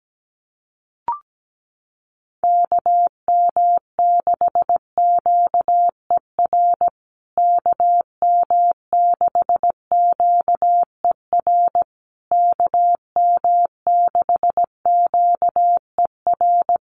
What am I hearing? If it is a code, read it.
Morse code, heard as a keyed tone, about 17 wpm, 700 Hz. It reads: KM6QER KM6QER KM6QER
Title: USA Call Signs Repeated 3x 17wpm